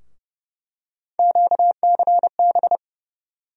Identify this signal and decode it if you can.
Morse code, heard as a keyed tone, about 30 wpm, 700 Hz. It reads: QCB
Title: USA Call Sign Postfixes - 3 Letters 30wpm